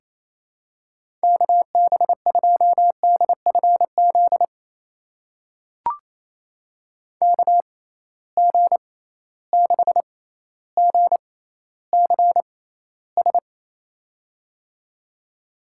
Morse code, heard as a keyed tone, about 28 wpm, spaced as for 14 wpm, 700 Hz. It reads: KB2DFZ KG6GCS
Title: USA Call Signs - 6 Characters - 6x Character Spacing - 28wpm